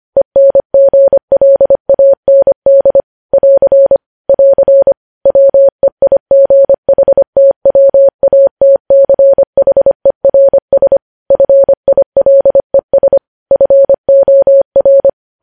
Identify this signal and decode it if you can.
Morse code, heard as a keyed tone, about 25 wpm, 550 Hz. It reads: ENGLAND <AR> <AR> WEIGHTWATCHERS FILES FOR
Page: news headlines in morse code at 25 wpm